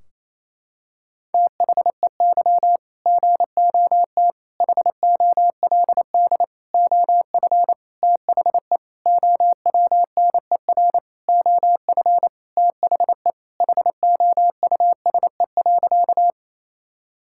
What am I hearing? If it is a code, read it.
Morse code, heard as a keyed tone, about 28 wpm, 700 Hz. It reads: THEY GOT HOLD OF THE OWNER OF THE HOUSE.
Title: Sentences from Top 1600 Words 28wpm